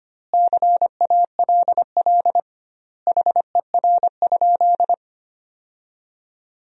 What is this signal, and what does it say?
Morse code, heard as a keyed tone, about 25 wpm, 700 Hz. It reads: CALL HER?
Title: 2x Word-Spacing - Sentences from Top 100 Words - 25wpm